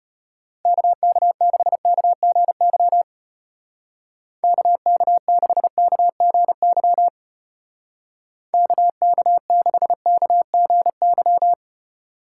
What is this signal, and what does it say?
Morse code, heard as a keyed tone, about 32 wpm, 700 Hz. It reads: KK6KGY KK6KGY KK6KGY